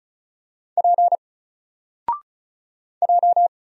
Morse code, heard as a keyed tone, about 35 wpm, 700 Hz. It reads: P J
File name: Single-Letter-Number-35wpm